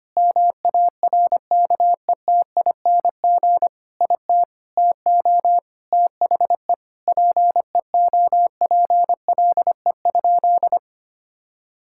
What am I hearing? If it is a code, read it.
Morse code, heard as a keyed tone, about 25 wpm, 700 Hz. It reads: MARKETING IT TO THE PEOPLE?
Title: Sentences from Top 1000 Words - Review Part 5 - 25wpm